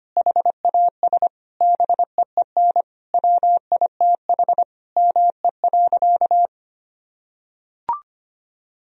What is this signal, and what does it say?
Morse code, heard as a keyed tone, about 25 wpm, 700 Hz. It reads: HAS BEEN WITH ME.
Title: Sentences from Top 1000 Words - Review Part 1 - 25wpm